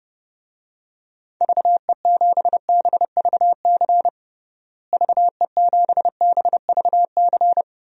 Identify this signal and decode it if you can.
Morse code, heard as a keyed tone, about 32 wpm, 700 Hz. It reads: VE7BVC VE7BVC
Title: DXCC-100 Call Signs - 6 Characters - Repeated 2x - 30wpm